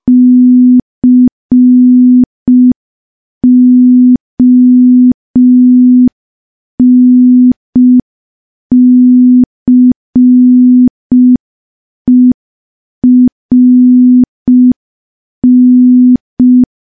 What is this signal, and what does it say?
Morse code, heard as a keyed tone, about 5 wpm, 250 Hz. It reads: CONCERN
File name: morse / morse-training-05WPM